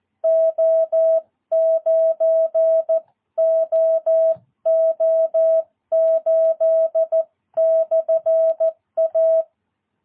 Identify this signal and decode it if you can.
Morse code, heard as a keyed tone, about 14 wpm, 660 Hz. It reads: O9OO8/A
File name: jh8cbh / 2022cw